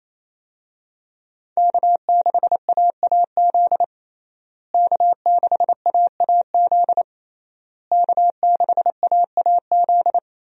Morse code, heard as a keyed tone, about 28 wpm, 700 Hz. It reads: K6AAZ K6AAZ K6AAZ
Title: USA Call Signs - 5 Characters - Repeated 3x - 28wpm